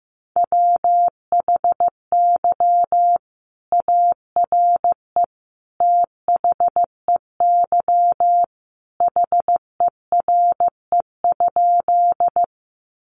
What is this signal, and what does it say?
Morse code, heard as a keyed tone, about 15 wpm, 700 Hz. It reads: WHY ARE THEY HERE?
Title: Sentences from Top 200 Words 15wpm